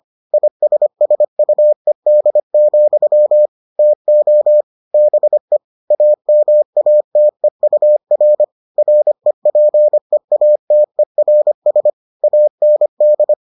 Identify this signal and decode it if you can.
Morse code, heard as a keyed tone, about 25 wpm, 600 Hz. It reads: ISSUED, TO BE AMATEUR REPEATERS AND